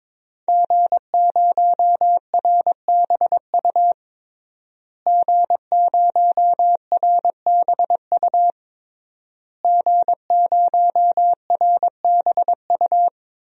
Morse code, heard as a keyed tone, about 22 wpm, 700 Hz. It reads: G0RBU G0RBU G0RBU
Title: DXCC-100 Call Signs - 5 Characters - Repeated 3x - 22wpm